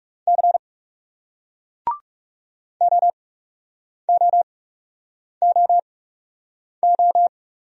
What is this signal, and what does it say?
Morse code, heard as a keyed tone, about 40 wpm, spaced as for 8 wpm, 700 Hz. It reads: C OOOO